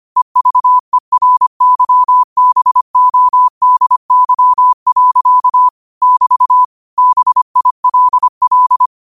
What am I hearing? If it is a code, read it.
Morse code, heard as a keyed tone, about 25 wpm, 1000 Hz. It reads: EVERYBODY. <BT> BILL
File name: 25wpm